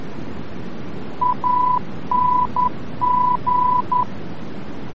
Morse code, heard as a keyed tone, about 11 wpm, 990 Hz, with noise. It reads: ANG
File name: france / mx / ANG-VOR-F